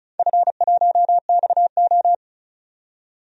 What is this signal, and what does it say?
Morse code, heard as a keyed tone, about 36 wpm, 700 Hz. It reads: F1XO
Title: DXCC-100 Call Signs - 4 Characters - 35wpm